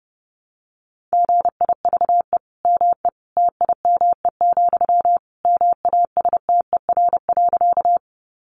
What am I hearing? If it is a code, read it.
Morse code, heard as a keyed tone, about 30 wpm, 700 Hz. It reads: GIVE ME TIME, MASTER.
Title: Sentences from Top 1300 Words 30wpm